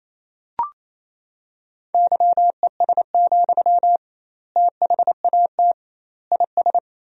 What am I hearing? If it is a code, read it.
Morse code, heard as a keyed tone, about 28 wpm, 700 Hz. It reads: YES, THAT IS
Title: 2x Word-Spacing - Sentences from Top 500 Words - 28wpm